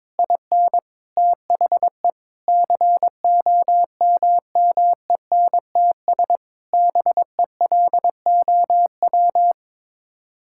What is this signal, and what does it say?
Morse code, heard as a keyed tone, about 22 wpm, 700 Hz. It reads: IN THE COMMENTS BELOW
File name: Sets of 9 Words - 22wpm